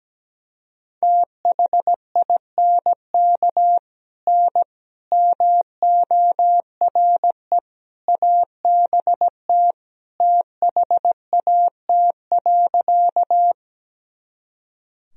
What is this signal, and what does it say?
Morse code, heard as a keyed tone, about 17 wpm, 700 Hz. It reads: THINK N MORE ABT THAT.
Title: Sentences from Top 200 Words and Abbreviations 17wpm